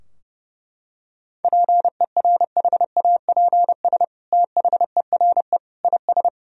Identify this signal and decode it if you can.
Morse code, heard as a keyed tone, about 30 wpm, 700 Hz. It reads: PERHAPS THERE IS
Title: Sentences from Top 1000 Words - Review Part 4 - 30wpm